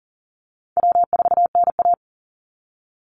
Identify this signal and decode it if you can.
Morse code, heard as a keyed tone, about 38 wpm, 700 Hz. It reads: W4NA
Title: Easy Call Signs - 40wpm